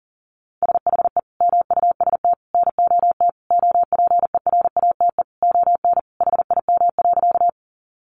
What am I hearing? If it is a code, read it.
Morse code, heard as a keyed tone, about 40 wpm, 700 Hz. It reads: SHE MUST NOT OPERATE ON HIM.
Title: Sentences from Top 1500 Words 40wpm